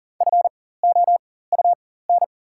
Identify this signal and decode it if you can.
Morse code, heard as a keyed tone, about 40 wpm, spaced as for 19 wpm, 690 Hz. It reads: FOUN
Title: 4x Character Spacing - Top 300 Words - 40wpm